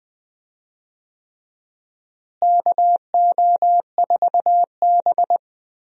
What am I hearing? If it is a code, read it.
Morse code, heard as a keyed tone, about 20 wpm, 700 Hz. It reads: KO4B
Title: USA Call Signs - 4 Characters - Repeated 3x - 20wpm